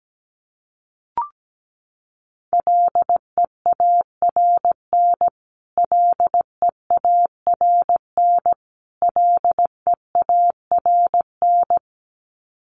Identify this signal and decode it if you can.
Morse code, heard as a keyed tone, about 17 wpm, 700 Hz. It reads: LEARN LEARN LEARN
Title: Top 500 Words Repeated 17wpm